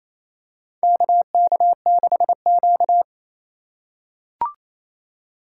Morse code, heard as a keyed tone, about 28 wpm, 700 Hz. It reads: KK6Q
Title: USA Call Signs - 2x Character Spacing - 28wpm